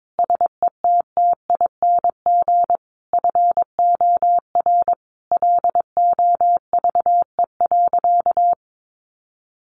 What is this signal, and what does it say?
Morse code, heard as a keyed tone, about 22 wpm, 700 Hz. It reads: SETTING FOR LOVE.
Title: Sentences from Top 1300 Words 22wpm